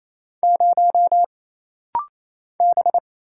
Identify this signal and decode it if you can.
Morse code, heard as a keyed tone, about 27 wpm, 700 Hz. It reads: 0 B
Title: Single Letter-Number Alt - Rapid-Fire 28wpm